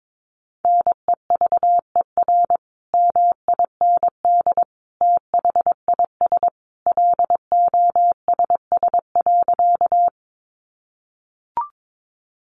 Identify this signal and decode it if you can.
Morse code, heard as a keyed tone, about 22 wpm, 700 Hz. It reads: NEVER MIND THIS LOSS.
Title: Sentences from Top 1400 Words 22wpm